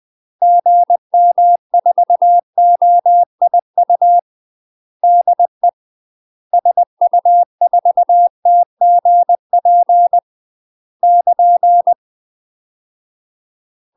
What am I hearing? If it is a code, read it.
Morse code, heard as a keyed tone, about 20 wpm, 700 Hz. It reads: GM4OIU DE SU4TGP <KN>
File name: UK MASTER QSOs TRAINING / 20WPM / UK MASTER QSOs 0071